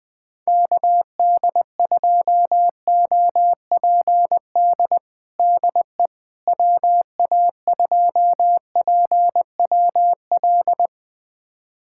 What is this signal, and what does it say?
Morse code, heard as a keyed tone, about 20 wpm, 700 Hz. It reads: KD2OPD DE WA2PWL
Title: USA Callsign DE Callsign 20wpm